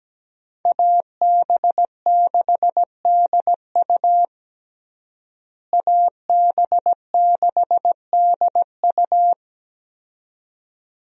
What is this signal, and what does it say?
Morse code, heard as a keyed tone, about 17 wpm, 700 Hz. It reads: AB6DU AB6DU
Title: USA Call Signs - 5 Characters - Repeated 2x - 17wpm